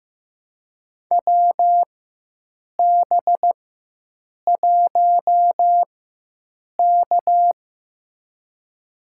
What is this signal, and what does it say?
Morse code, heard as a keyed tone, about 15 wpm, spaced as for 7 wpm, 700 Hz. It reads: WB1K
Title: USA Call Signs - 4 Characters - 4x Character Spacing - 15wpm